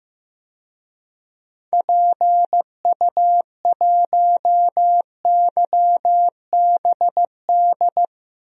Morse code, heard as a keyed tone, about 15 wpm, 700 Hz. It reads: PU1YBD
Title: DXCC-100 Call Signs - 6 Characters - 15wpm